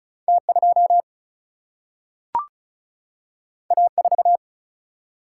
Morse code, heard as a keyed tone, about 36 wpm, 700 Hz. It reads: T2 A4
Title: Single-Letter Single-Number 35wpm